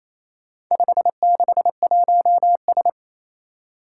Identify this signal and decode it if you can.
Morse code, heard as a keyed tone, about 27 wpm, 700 Hz. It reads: 561S
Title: Four Challenging Letter-Number Combinations 28wpm